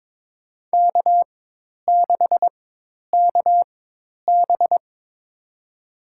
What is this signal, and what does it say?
Morse code, heard as a keyed tone, about 23 wpm, spaced as for 10 wpm, 700 Hz. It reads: K6KB
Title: USA Call Signs - 4 Characters - 4x Character Spacing - 22wpm